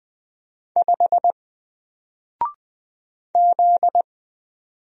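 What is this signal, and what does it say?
Morse code, heard as a keyed tone, about 20 wpm, 700 Hz. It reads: 5 Z
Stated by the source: Single letter and number Morse drill at 20 wpm, from N4LCD.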